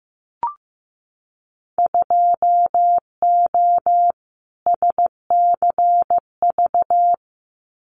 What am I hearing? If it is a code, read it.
Morse code, heard as a keyed tone, about 15 wpm, 700 Hz. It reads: 2O SCV
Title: Winter Field Day Exchanges Encore 15wpm